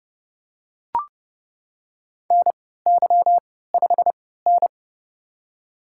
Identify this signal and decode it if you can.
Morse code, heard as a keyed tone, about 31 wpm, spaced as for 17 wpm, 700 Hz. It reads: NY5N
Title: USA Call Signs - 3x Character Spacing - 30wpm